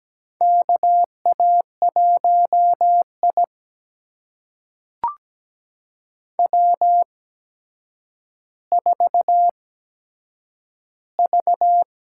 Morse code, heard as a keyed tone, about 17 wpm, spaced as for 8 wpm, 700 Hz. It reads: KA1I W4V
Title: USA Call Signs - 4 Characters - 8x Character Spacing - 17wpm